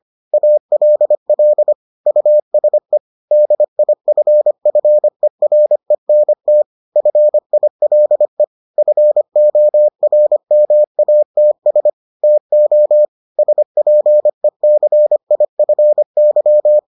Morse code, heard as a keyed tone, about 25 wpm, 600 Hz. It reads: ALL USE DIFFERENT FILE FORMATS TO SPECIFY